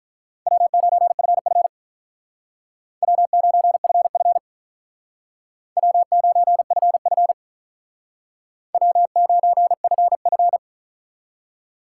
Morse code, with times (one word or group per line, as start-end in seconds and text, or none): 0.5-1.7 W9FF
3.0-4.4 W9FF
5.8-7.3 W9FF
8.7-10.6 W9FF